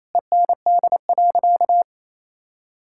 Morse code, heard as a keyed tone, about 28 wpm, 700 Hz. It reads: END.
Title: Sentences from Top 1000 Words - Review Part 5 - 28wpm